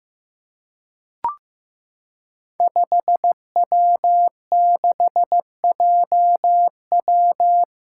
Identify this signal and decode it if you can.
Morse code, heard as a keyed tone, about 15 wpm, 700 Hz. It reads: 5W6JW
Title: Five Challenging Letter-Number Combinations 15wpm